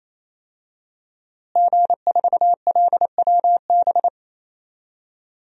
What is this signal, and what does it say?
Morse code, heard as a keyed tone, about 28 wpm, 700 Hz. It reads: G4LWB
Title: DXCC-100 Call Signs - 5 Characters - Repeated 3x - 28wpm